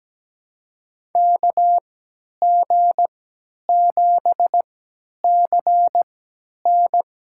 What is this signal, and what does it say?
Morse code, heard as a keyed tone, about 17 wpm, spaced as for 10 wpm, 700 Hz. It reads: KG7CN
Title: USA Call Signs - 6 Characters - 3x Character Spacing - 17wpm